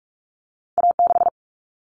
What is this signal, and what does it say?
Morse code, heard as a keyed tone, about 44 wpm, 700 Hz. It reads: A6
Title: Common Cut Numbers - 2-Digit 45wpm